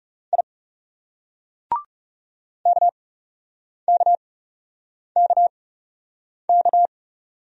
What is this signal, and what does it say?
Morse code, heard as a keyed tone, about 38 wpm, 700 Hz. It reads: I K K K K